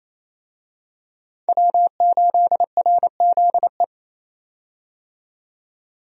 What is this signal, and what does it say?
Morse code, heard as a keyed tone, about 28 wpm, 700 Hz. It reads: W8RZE
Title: USA Call Signs - 5 Characters - 28wpm